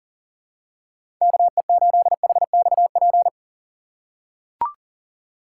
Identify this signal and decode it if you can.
Morse code, heard as a keyed tone, about 37 wpm, 700 Hz. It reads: KE8HXP
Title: USA Call Signs - 2x Character Spacing - 40wpm